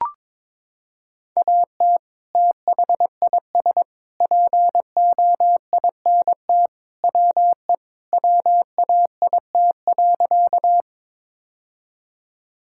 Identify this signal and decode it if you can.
Morse code, heard as a keyed tone, about 22 wpm, 700 Hz. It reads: AT THIS POINT WE WAIT.
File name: practice / 22wpm / Sentences with 3-Word Phrases - 22wpm